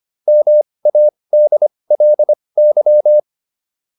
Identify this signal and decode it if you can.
Morse code, heard as a keyed tone, about 25 wpm, spaced as for 20 wpm, 600 Hz. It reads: MADLY